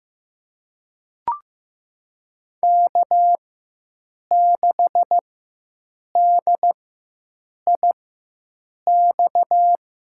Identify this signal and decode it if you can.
Morse code, heard as a keyed tone, about 15 wpm, spaced as for 7 wpm, 700 Hz. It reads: K6DIX
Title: USA Call Signs - 5 Characters - 4x Character Spacing - 15wpm